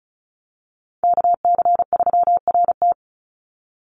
Morse code, heard as a keyed tone, about 34 wpm, 700 Hz. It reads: KC3RT
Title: NA SOTA Chaser Callsigns 35wpm